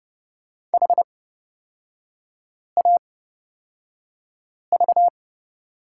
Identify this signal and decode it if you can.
Morse code, heard as a keyed tone, about 31 wpm, spaced as for 5 wpm, 700 Hz. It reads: HAV
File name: Lesson 38e - Letter q - Three Characters 30wpm